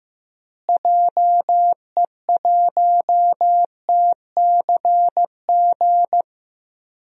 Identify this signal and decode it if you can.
Morse code, heard as a keyed tone, about 15 wpm, 700 Hz. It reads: JE1TCG
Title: DXCC-100 Call Signs - 6 Characters - Repeated 3x - 15wpm